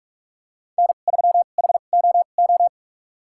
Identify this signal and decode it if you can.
Morse code, heard as a keyed tone, about 42 wpm, spaced as for 31 wpm, 700 Hz. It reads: N3HOO
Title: USA Call Signs - 5 Characters - 2x Character Spacing - 45wpm